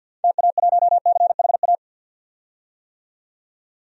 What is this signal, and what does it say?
Morse code, heard as a keyed tone, about 52 wpm, 700 Hz. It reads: TA1CHA
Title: DXCC-100 Call Signs - 6 Characters - 50wpm